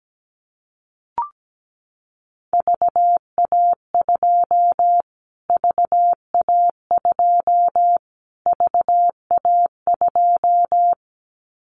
Morse code, heard as a keyed tone, about 17 wpm, 700 Hz. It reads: VA2 VA2 VA2
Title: Callsign Prefixes for Top 25 DXCC - Repeated 3x - 17wpm